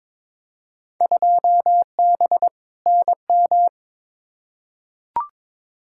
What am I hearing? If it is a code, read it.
Morse code, heard as a keyed tone, about 22 wpm, 700 Hz. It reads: 2B NM
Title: Field Day Exchanges Encore - 22wpm